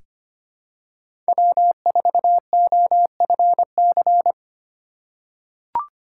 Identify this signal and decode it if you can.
Morse code, heard as a keyed tone, about 25 wpm, 700 Hz. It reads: W4OFC
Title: USA Call Signs - 5 Characters - 4x Character Spacing - 25wpm